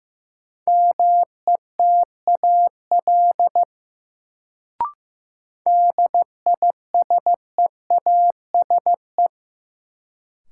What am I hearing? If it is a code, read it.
Morse code, heard as a keyed tone, about 15 wpm, 700 Hz. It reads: METAL DISEASE
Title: Sentences from Top 1000 Words - Review Part 6 - New Word Drill - 15wpm